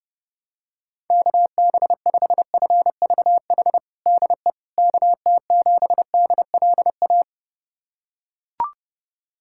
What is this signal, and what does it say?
Morse code, heard as a keyed tone, about 30 wpm, 700 Hz. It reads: KB5FVH DE KT7DLA
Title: USA Callsign DE Callsign 30wpm